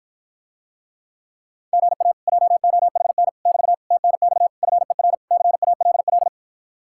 Morse code, heard as a keyed tone, about 54 wpm, 700 Hz. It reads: GA JOHN <BT> TNX FER CALL